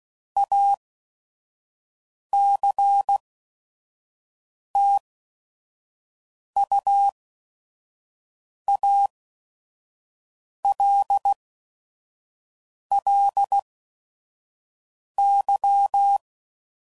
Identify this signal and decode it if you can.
Morse code, heard as a keyed tone, about 16 wpm, spaced as for 5 wpm, 800 Hz. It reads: ACTUALLY
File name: CW Code Groups @5WPM / QSOs / kc7sac